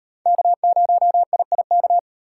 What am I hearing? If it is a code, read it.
Morse code, heard as a keyed tone, about 38 wpm, 700 Hz. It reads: K0IIK